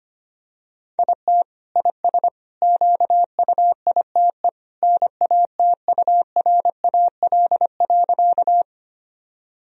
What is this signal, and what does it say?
Morse code, heard as a keyed tone, about 25 wpm, 700 Hz. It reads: IT IS QUITE NATURAL.
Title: Sentences from Top 1000 Words - Review Part 4 - 25wpm